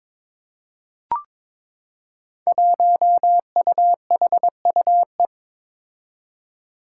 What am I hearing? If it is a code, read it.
Morse code, heard as a keyed tone, about 22 wpm, 700 Hz. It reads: 1UHUE
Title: Five Challenging Letter-Number Combinations 22wpm